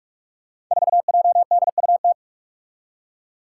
Morse code, heard as a keyed tone, about 46 wpm, 700 Hz. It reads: 4JDUT
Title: Five Challenging Letter-Number Combinations 45wpm